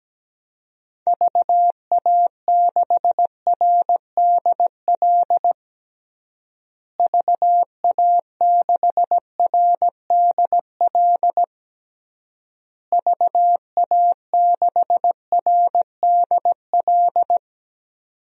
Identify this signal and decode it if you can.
Morse code, heard as a keyed tone, about 17 wpm, 700 Hz. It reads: VA6RDL VA6RDL VA6RDL
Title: DXCC-100 Call Signs - 6 Characters - Repeated 3x - 17wpm